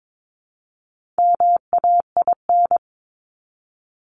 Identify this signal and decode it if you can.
Morse code, heard as a keyed tone, about 22 wpm, 700 Hz. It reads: MAIN
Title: Top 300 Words 22wpm